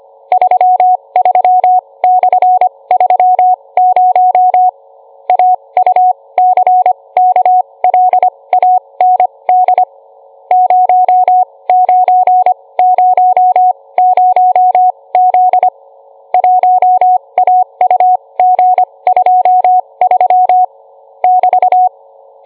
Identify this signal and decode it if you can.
Morse code, heard as a keyed tone, about 25 wpm, spaced as for 20 wpm, 730 Hz. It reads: 33/30 AUCKLAND 0900Z 1AUG23 <BT>